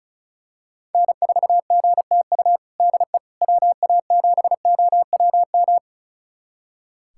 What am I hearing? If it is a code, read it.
Morse code, heard as a keyed tone, about 36 wpm, 700 Hz. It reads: N4GTU DE WA7OWM
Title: USA Callsign DE Callsign 35wpm